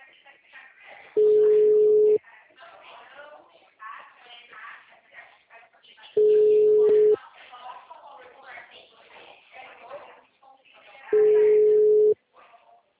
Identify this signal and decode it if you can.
Ringback tone (Europe, 425 Hz), with speech.